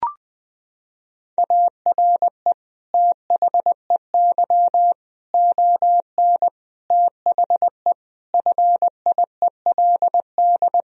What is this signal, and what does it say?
Morse code, heard as a keyed tone, about 19 wpm, 700 Hz. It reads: ARE THEY ON THE FIELD
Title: Sentences from Top 1000 Words - Review Part 2 - 20wpm